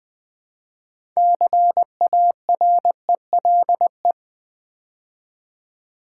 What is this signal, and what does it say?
Morse code, heard as a keyed tone, about 20 wpm, 700 Hz. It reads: CARELE
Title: Build A Word - 2 Letters At A Time - 8-Letter Words - 20wpm